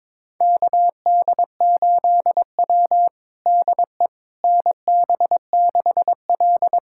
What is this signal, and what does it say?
Morse code, heard as a keyed tone, about 21 wpm, 700 Hz. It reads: KD8W DE NB6L
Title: USA Callsign DE Callsign - 4 Characters - 22wpm